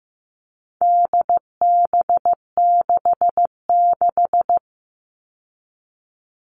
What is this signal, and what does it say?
Morse code, heard as a keyed tone, about 15 wpm, 700 Hz. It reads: DB66
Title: All Cut Numbers - 4-Digit 15wpm